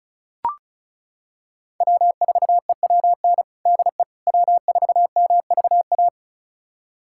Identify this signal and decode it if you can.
Morse code, heard as a keyed tone, about 36 wpm, 700 Hz. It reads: W4EWN DE W4MVA
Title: USA Callsign DE Callsign - 5 Characters - 35wpm